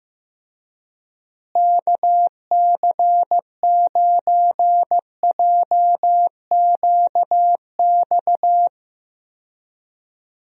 Morse code, heard as a keyed tone, about 14 wpm, 700 Hz. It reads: KC9JQX
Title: USA Call Signs - 6 Characters - Repeated 3x - 15wpm